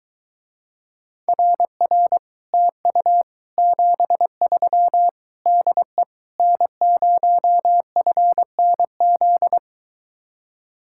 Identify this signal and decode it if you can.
Morse code, heard as a keyed tone, about 22 wpm, 700 Hz. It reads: RR TU 73 DE N0FNZ